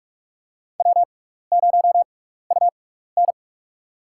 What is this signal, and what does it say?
Morse code, heard as a keyed tone, about 46 wpm, spaced as for 15 wpm, 700 Hz. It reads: W0UN
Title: USA Call Signs - 4 Characters - 6x Character Spacing - 45wpm